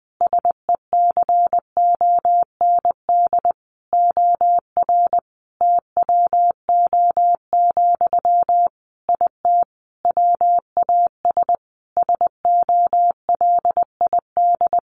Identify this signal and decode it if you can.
Morse code, heard as a keyed tone, about 20 wpm, 700 Hz. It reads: SECOND OR TWO, IT WAS SOLID
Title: Sentences from Top 1500 Words 20wpm